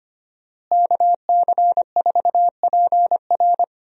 Morse code, heard as a keyed tone, about 25 wpm, 700 Hz. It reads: KC4PR
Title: USA Call Signs - 5 Characters - Repeated 2x - 25wpm